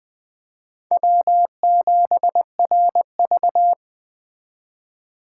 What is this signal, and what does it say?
Morse code, heard as a keyed tone, about 20 wpm, 700 Hz. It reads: W7RV
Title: 2023 SOTA Chaser Call Signs - 20wpm